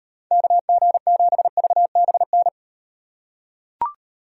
Morse code, heard as a keyed tone, about 38 wpm, 700 Hz. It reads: KG7VBN